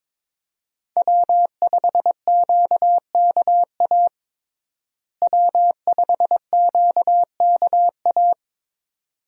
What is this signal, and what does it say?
Morse code, heard as a keyed tone, about 22 wpm, 700 Hz. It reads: W5QKA W5QKA
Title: USA Call Signs - 5 Characters - Repeated 2x - 22wpm